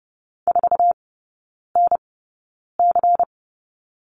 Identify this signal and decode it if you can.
Morse code, heard as a keyed tone, about 30 wpm, spaced as for 9 wpm, 700 Hz. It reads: 4NC
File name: Lesson 17e - Letter u - Three Characters 30wpm